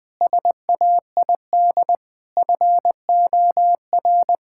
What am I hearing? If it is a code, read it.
Morse code, heard as a keyed tone, about 20 wpm, 700 Hz. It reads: SAID FOR
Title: Sentences with 7-Word Phrases - 20wpm